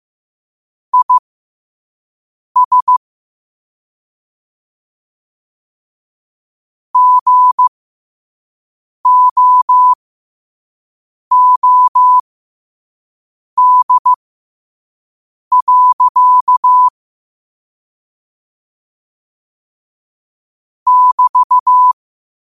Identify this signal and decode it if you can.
Morse code, heard as a keyed tone, about 15 wpm, spaced as for 5 wpm, 1000 Hz. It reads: IS GOOD. <BT>
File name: 5wpm